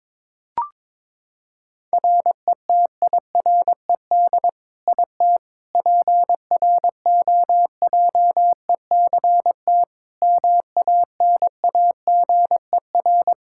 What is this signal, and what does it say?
Morse code, heard as a keyed tone, about 21 wpm, 700 Hz. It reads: RETIRED IT PROJECT MANAGER
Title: Practice QSO - Exchange 3 - More Info - 22wpm